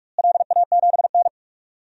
Morse code, heard as a keyed tone, about 42 wpm, 700 Hz. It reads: LA7N
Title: DXCC-100 Call Signs - 4 Characters - Repeated 3x - 45wpm